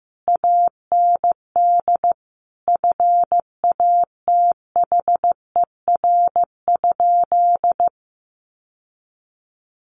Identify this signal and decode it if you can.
Morse code, heard as a keyed tone, about 15 wpm, 700 Hz. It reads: AND FATHER?
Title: Sentences from Top 1200 Words 15wpm